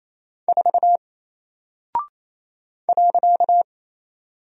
Morse code, heard as a keyed tone, about 27 wpm, 700 Hz. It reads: 4 .
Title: Single Character (Letter:Number:Punctuation:Prosign) Alt 28wpm